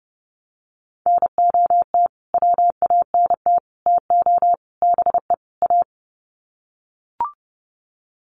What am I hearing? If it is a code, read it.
Morse code, heard as a keyed tone, about 30 wpm, 700 Hz. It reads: NOT WANT TO BE A
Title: Sets of 5 Words Encore 30wpm